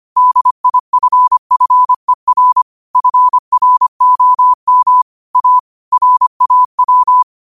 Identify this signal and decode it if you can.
Morse code, heard as a keyed tone, about 25 wpm, 1000 Hz. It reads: DIFFER FROM A RAW